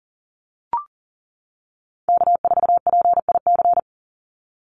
Morse code, heard as a keyed tone, about 40 wpm, 700 Hz. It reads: K4PIC
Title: NA SOTA Chaser Callsigns 40wpm